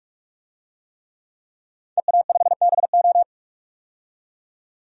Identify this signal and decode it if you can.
Morse code, heard as a keyed tone, about 41 wpm, 700 Hz. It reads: EA5BO
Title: DXCC-100 Call Signs - 5 Characters - 45wpm